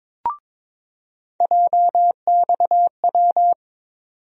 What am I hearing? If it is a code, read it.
Morse code, heard as a keyed tone, about 22 wpm, 700 Hz. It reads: JXW
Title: USA Call Sign Postfixes - 3 Letters 22wpm